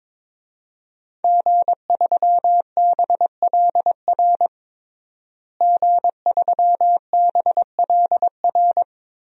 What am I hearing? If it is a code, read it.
Morse code, heard as a keyed tone, about 22 wpm, 700 Hz. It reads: G3BLR G3BLR
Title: DXCC-100 Call Signs - 5 Characters - Repeated 3x - 22wpm